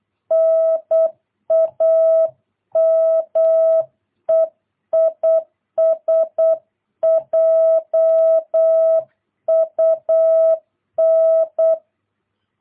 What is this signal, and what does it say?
Morse code, heard as a keyed tone, about 8 wpm, 650 Hz. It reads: NAME IS JUN
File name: jh8cbh / 2022cw